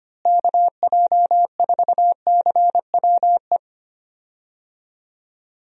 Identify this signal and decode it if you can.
Morse code, heard as a keyed tone, about 25 wpm, 700 Hz. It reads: KJ4CWE
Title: USA Call Signs - 6 Characters - Repeated 2x - 25wpm